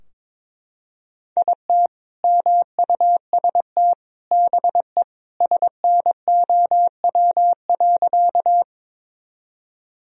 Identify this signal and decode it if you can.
Morse code, heard as a keyed tone, about 22 wpm, 700 Hz. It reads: IT MUST BE SNOW.
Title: Sentences from Top 1000 Words - Review Part 3 - 22wpm